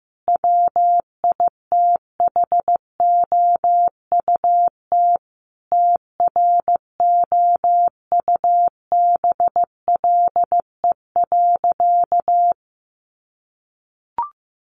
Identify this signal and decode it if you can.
Morse code, heard as a keyed tone, about 15 wpm, 700 Hz. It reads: WITHOUT TROUBLE.
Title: Sentences from Top 1200 Words 15wpm